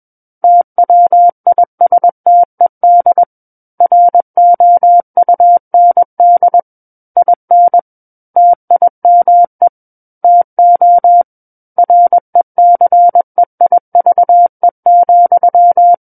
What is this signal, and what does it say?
Morse code, heard as a keyed tone, about 21 wpm, 700 Hz. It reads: TWISTED ROUND IN TIME TO RECEIVE,
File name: Book One / 12-Chapter 12 - 21 WPM